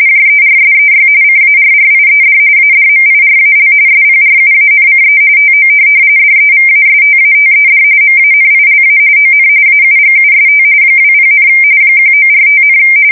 RTTY, about 45.45 baud. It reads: JERSEY CARRIED ADMIRAL SPRUANCE TO DEVASTATED MANILA ON AUGUST 20, 1945,
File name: fsk170 / USS-New-Jersey